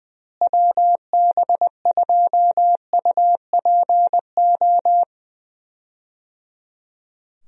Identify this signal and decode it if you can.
Morse code, heard as a keyed tone, about 20 wpm, 700 Hz. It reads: WB2UPO
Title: USA Call Signs - 6 Characters - Repeated 3x - 20wpm